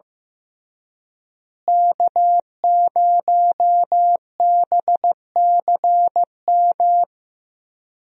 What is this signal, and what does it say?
Morse code, heard as a keyed tone, about 15 wpm, 700 Hz. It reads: K0BCM
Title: USA Call Signs - 5 Characters - Repeated 2x - 15wpm